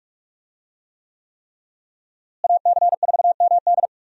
Morse code, heard as a keyed tone, about 42 wpm, 700 Hz. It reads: AC4MD
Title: USA Call Signs - 5 Characters - Repeated 3x - 45wpm